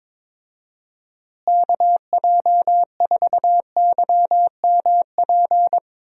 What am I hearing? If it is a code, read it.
Morse code, heard as a keyed tone, about 21 wpm, 700 Hz. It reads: KJ4YMP
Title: USA Call Signs - 6 Characters - Repeated 3x - 22wpm